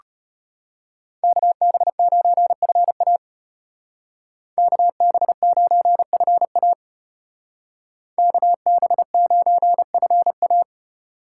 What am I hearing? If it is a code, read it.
Morse code, heard as a keyed tone, about 32 wpm, 700 Hz. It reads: KB9FA KB9FA KB9FA